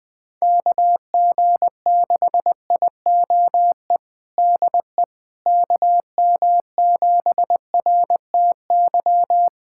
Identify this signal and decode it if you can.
Morse code, heard as a keyed tone, about 20 wpm, 700 Hz. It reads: KG6IOE DE KM7RTY
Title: USA Callsign DE Callsign 20wpm